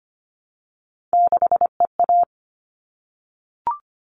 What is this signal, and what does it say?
Morse code, heard as a keyed tone, about 25 wpm, 700 Hz. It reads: 6EA
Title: All Cut Numbers - 3-Digit 25wpm